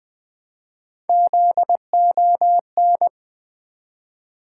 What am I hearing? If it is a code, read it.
Morse code, heard as a keyed tone, about 20 wpm, 700 Hz. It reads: ZON
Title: Build A Word - 1 Letter At A Time - 4-Letter Words - 20wpm